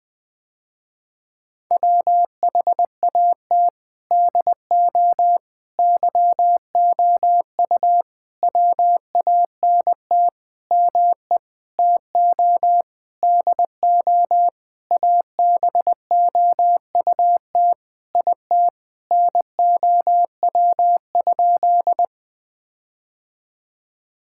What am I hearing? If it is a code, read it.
Morse code, heard as a keyed tone, about 20 wpm, 700 Hz. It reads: WHAT DO YOU WANT ME TO DO ABOUT IT NOW?
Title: Sentences with 7-Word Phrases - Spoken Beginning - 20wpm